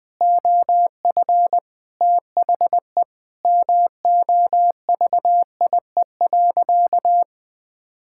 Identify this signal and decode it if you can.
Morse code, heard as a keyed tone, about 20 wpm, 700 Hz. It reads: OF THE MOVIE.
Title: Sentences with 4-Word Phrases - 20wpm